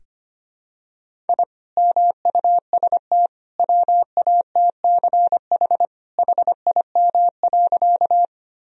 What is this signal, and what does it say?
Morse code, heard as a keyed tone, about 25 wpm, 700 Hz. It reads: I MUST WATCH HIM.
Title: Sentences from Top 1000 Words - Review Part 1 - 25wpm